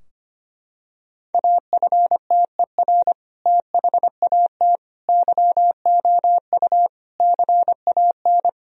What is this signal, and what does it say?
Morse code, heard as a keyed tone, about 25 wpm, 700 Hz. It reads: AFTER THAT YOU CAN
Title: Sentences from Top 1000 Words - Review Part 2 - 25wpm